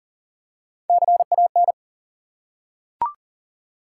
Morse code, heard as a keyed tone, about 39 wpm, 700 Hz. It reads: CAN
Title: Top 180 QSO Elements 40wpm